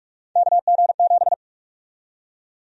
Morse code, heard as a keyed tone, about 46 wpm, 700 Hz. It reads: KG7
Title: USA Call Sign Prefixes - Repeated 2x - 45wpm